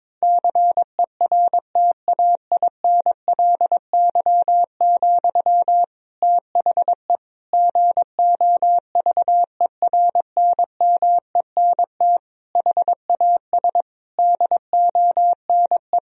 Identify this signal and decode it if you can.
Morse code, heard as a keyed tone, about 22 wpm, 700 Hz. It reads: CERTAINLY, THE GOVERNMENT HAS DONE
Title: Sentences from Top 1000 Words - Review Part 3 - 22wpm